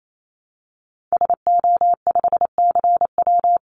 Morse code, heard as a keyed tone, about 28 wpm, 700 Hz. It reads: SO5CW
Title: Call Signs Repeated 2x - 28wpm